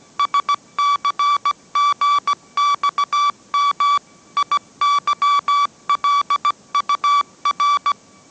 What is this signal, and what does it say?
Morse code, heard as a keyed tone, about 17 wpm, 1150 Hz. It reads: SCGXM IYLUR
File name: mrx / mrx080 18wpm